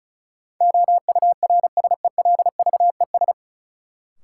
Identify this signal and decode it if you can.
Morse code, heard as a keyed tone, about 36 wpm, 700 Hz. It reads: OURSELVES
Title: Sentences from Top 1600 Words - New Word Drill - 35wpm